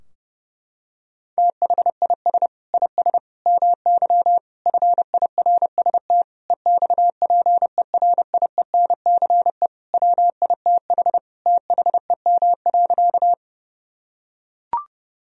Morse code, heard as a keyed tone, about 30 wpm, 700 Hz. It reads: THIS IS MY FIRST EXPERIENCE WITH THEM.
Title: Sentences from Top 1000 Words - Review Part 3 - 30wpm